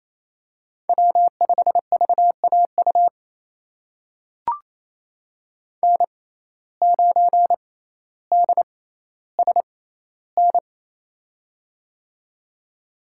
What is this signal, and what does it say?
Morse code, heard as a keyed tone, about 28 wpm, spaced as for 14 wpm, 700 Hz. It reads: W5VAU N9DSN
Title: USA Call Signs - 5 Characters - 6x Character Spacing - 28wpm